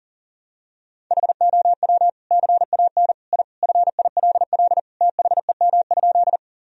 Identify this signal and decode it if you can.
Morse code, heard as a keyed tone, about 37 wpm, 700 Hz. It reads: HOW CAN I FILL THEM?
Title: Sentences from Top 1000 Words - Review Part 3 - 40wpm